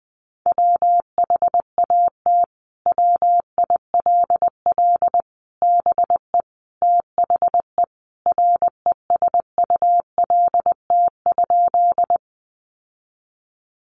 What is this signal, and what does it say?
Morse code, heard as a keyed tone, about 20 wpm, 700 Hz. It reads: WHAT WILL BE THE RESULT?
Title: Sentences from Top 700 Words 20wpm